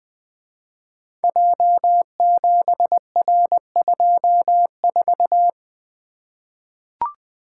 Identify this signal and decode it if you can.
Morse code, heard as a keyed tone, about 20 wpm, 700 Hz. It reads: J7R24
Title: Code Groups - 5 Letter:Numbers - 20wpm